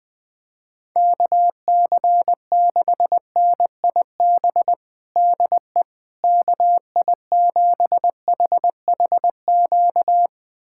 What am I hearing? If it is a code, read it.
Morse code, heard as a keyed tone, about 21 wpm, 700 Hz. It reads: KC6NIB DE KI7HHQ
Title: USA Callsign DE Callsign - 6 Characters - 20wpm